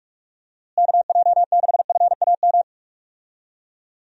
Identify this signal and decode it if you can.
Morse code, heard as a keyed tone, about 46 wpm, 700 Hz. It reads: KJ6FAM
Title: USA Call Signs - 6 Characters - Repeated 2x - 45wpm